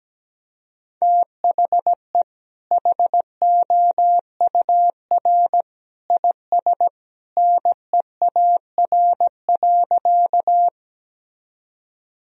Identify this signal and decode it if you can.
Morse code, heard as a keyed tone, about 17 wpm, 700 Hz. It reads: THE HOUR IS NEAR.
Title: Sentences from Top 1000 Words - Review Part 1 - 17wpm